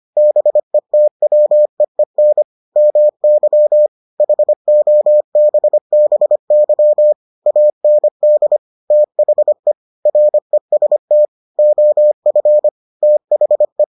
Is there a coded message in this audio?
Yes: Morse code from 0.2 to 13.8 s, heard as a keyed tone, about 25 wpm, 600 Hz. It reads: BETWEEN MY HOBBY AND THE REST OF THE